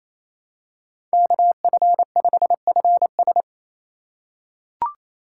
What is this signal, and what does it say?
Morse code, heard as a keyed tone, about 28 wpm, 700 Hz. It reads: KF5FS
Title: USA Call Signs - 5 Characters - 4x Character Spacing - 28wpm